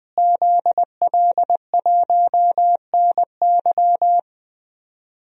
Morse code, heard as a keyed tone, about 20 wpm, 700 Hz. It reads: ZL1NY
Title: DXCC-100 Call Signs - 5 Characters - Repeated 3x - 20wpm